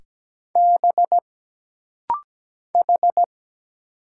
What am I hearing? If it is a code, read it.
Morse code, heard as a keyed tone, about 17 wpm, 700 Hz. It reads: B H
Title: Single Letters - Rapid-Fire 17wpm